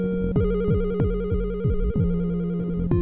There is no code.